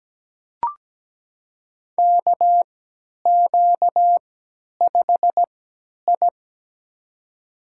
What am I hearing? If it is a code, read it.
Morse code, heard as a keyed tone, about 17 wpm, spaced as for 10 wpm, 700 Hz. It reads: KQ5I
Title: USA Call Signs - 3x Character Spacing - 17wpm